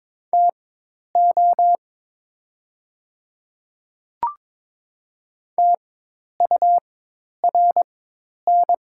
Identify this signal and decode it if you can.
Morse code, heard as a keyed tone, about 22 wpm, spaced as for 10 wpm, 700 Hz. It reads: TO TURN
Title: 4x Character Spacing - Top 200 Words - 22wpm